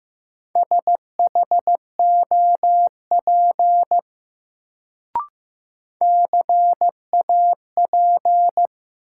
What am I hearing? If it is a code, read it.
Morse code, heard as a keyed tone, about 15 wpm, 700 Hz. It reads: SHOP CAP
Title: Sentences from Top 1600 Words - New Word Drill Alt - 15wpm